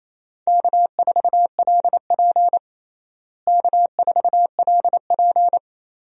Morse code, heard as a keyed tone, about 28 wpm, 700 Hz. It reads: K4LP K4LP
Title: USA Call Signs - 4 Characters - Repeated 2x - 28wpm